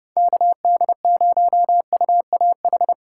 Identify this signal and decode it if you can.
Morse code, heard as a keyed tone, about 28 wpm, 700 Hz. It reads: KD0UAH
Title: Speed Racing - USA Call Signs 25wpm